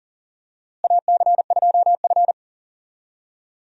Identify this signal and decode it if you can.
Morse code, heard as a keyed tone, about 37 wpm, 700 Hz. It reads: AC2F
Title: USA Call Signs - 4 Characters - Repeated 3x - 40wpm